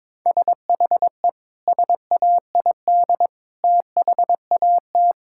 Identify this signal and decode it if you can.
Morse code, heard as a keyed tone, about 22 wpm, 700 Hz. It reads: SHE SAID THAT
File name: practice / 22wpm / Sentences with 3-Word Phrases - 22wpm